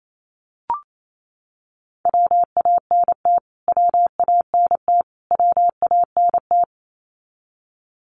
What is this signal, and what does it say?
Morse code, heard as a keyed tone, about 28 wpm, 700 Hz. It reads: WANT WANT WANT
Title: Top 200 Words Repeated 28wpm